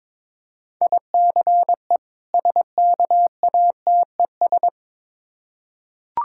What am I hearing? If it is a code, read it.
Morse code, heard as a keyed tone, about 22 wpm, 690 Hz. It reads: ICE SKATES
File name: Difficult Words with Hints - 22wpm